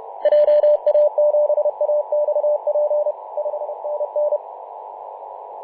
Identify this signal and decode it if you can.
Morse code, heard as a keyed tone, about 32 wpm, 600 Hz, with radio noise. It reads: JA7AXP 5NN
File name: Mar/20 2349z FW5JJ 28.011MHz CW